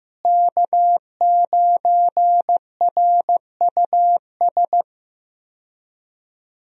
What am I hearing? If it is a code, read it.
Morse code, heard as a keyed tone, about 15 wpm, 700 Hz. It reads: K9RUS
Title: USA Call Signs - 5 Characters - Repeated 3x - 15wpm